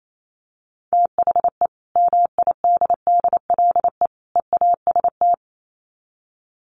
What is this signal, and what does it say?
Morse code, heard as a keyed tone, about 28 wpm, 700 Hz. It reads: THE MIDDLE EAST
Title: Sets of 3 Words 28wpm